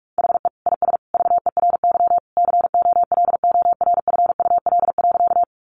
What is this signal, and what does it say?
Morse code, heard as a keyed tone, about 44 wpm, 700 Hz. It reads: HE IS VERY COLORFUL.
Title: Easy Phrases 45wpm